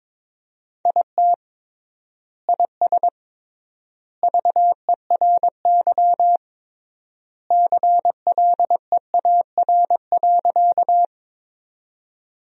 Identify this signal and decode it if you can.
Morse code, heard as a keyed tone, about 22 wpm, 700 Hz. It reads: IT IS VERY CLEAR.
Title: 3x Word-Spacing - Sentences from Top 500 Words - 22wpm